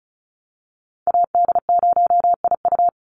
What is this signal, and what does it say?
Morse code, heard as a keyed tone, about 34 wpm, 700 Hz. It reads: AD0IU
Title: Call Signs - 35wpm